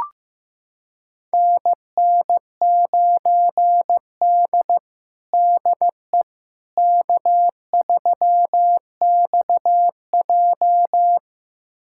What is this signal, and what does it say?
Morse code, heard as a keyed tone, about 15 wpm, 700 Hz. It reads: NN9D DE K3XJ
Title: USA Callsign DE Callsign - 4 Characters - 15wpm